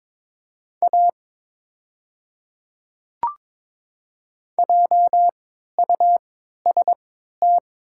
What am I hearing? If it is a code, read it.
Morse code, heard as a keyed tone, about 22 wpm, spaced as for 13 wpm, 700 Hz. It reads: A JUST
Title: 3x Character Spacing - Top 300 Words - 22wpm